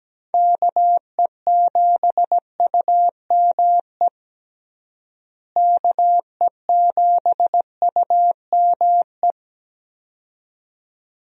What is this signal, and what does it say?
Morse code, heard as a keyed tone, about 17 wpm, 700 Hz. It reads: KE7UME KE7UME
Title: USA Call Signs - 6 Characters - Repeated 2x - 17wpm